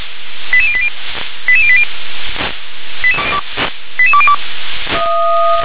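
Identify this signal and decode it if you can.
Five-tone selective call (ZVEI), heard as quick runs of five tones, tones of about 70 ms. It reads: 88080 88080 88202
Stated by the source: ZVEI five-tone selective call received on SDR.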